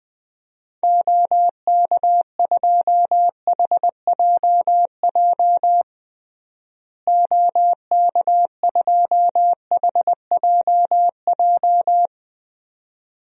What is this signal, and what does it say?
Morse code, heard as a keyed tone, about 20 wpm, 700 Hz. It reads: OK2HJJ OK2HJJ
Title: DXCC-100 Call Signs - 6 Characters - Repeated 2x - 20wpm